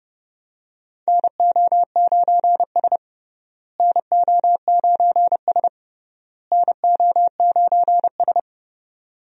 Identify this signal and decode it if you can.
Morse code, heard as a keyed tone, about 30 wpm, 700 Hz. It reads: NO9S NO9S NO9S
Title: USA Call Signs - 4 Characters - Repeated 3x - 30wpm